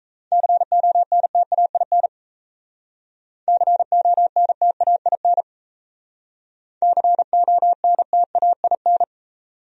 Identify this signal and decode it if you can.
Morse code, heard as a keyed tone, about 32 wpm, 700 Hz. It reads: CONTAIN CONTAIN CONTAIN
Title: Speed-Racing Top 500 Words 28wpm